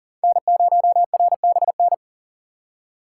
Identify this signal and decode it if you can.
Morse code, heard as a keyed tone, about 40 wpm, 700 Hz. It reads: N0RBN
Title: USA Call Signs - 5 Characters - 4x Character Spacing - 40wpm